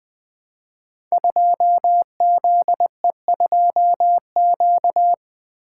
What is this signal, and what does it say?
Morse code, heard as a keyed tone, about 20 wpm, 700 Hz. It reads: 2ZE2Q
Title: Code Groups - 5 Letter:Numbers - 20wpm